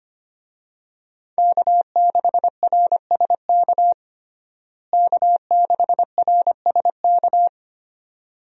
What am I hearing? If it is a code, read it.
Morse code, heard as a keyed tone, about 25 wpm, 700 Hz. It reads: K6RSK K6RSK
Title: USA Call Signs - 5 Characters - Repeated 2x - 25wpm